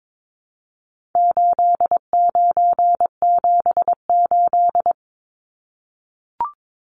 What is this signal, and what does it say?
Morse code, heard as a keyed tone, about 22 wpm, 700 Hz. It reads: 8978
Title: Numbers - 4-Digit 22wpm